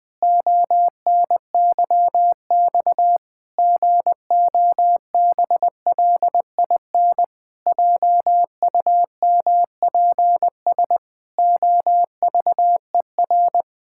Morse code, heard as a keyed tone, about 21 wpm, 700 Hz. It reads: ONYX GOBLIN JUMPS OVER
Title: Pangram Challenge v2 - 20wpm